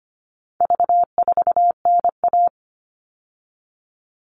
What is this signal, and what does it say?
Morse code, heard as a keyed tone, about 25 wpm, 700 Hz. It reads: V4NA
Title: All Cut Numbers - 4-Digit 25wpm